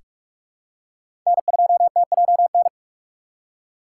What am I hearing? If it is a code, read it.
Morse code, heard as a keyed tone, about 46 wpm, 700 Hz. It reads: N2TJN
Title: USA Call Signs - 5 Characters - 2x Character Spacing - 45wpm